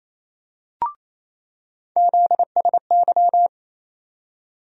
Morse code, heard as a keyed tone, about 27 wpm, 700 Hz. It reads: ZSY
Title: USA Call Sign Postfixes - 3 Letters 28wpm